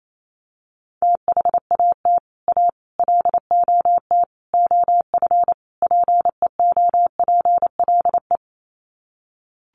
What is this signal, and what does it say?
Morse code, heard as a keyed tone, about 28 wpm, 700 Hz. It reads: THAT A LOT OF PEOPLE
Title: Sets of 5 Words 28wpm